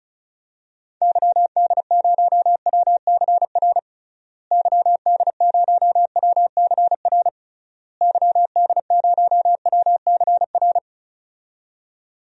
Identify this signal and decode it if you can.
Morse code, heard as a keyed tone, about 36 wpm, 700 Hz. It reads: YD0WCR YD0WCR YD0WCR
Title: DXCC-100 Call Signs - 6 Characters - Repeated 3x - 35wpm